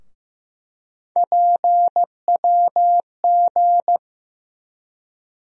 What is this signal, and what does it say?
Morse code, heard as a keyed tone, about 15 wpm, 700 Hz. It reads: PWG
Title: USA Call Sign Postfixes - 3 Letters 15wpm